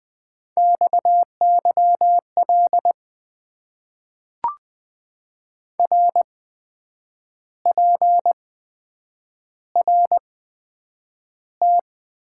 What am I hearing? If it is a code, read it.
Morse code, heard as a keyed tone, about 20 wpm, spaced as for 8 wpm, 700 Hz. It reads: XYL RPRT
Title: Top 90 QSO Elements - 8x Character Spacing - 20wpm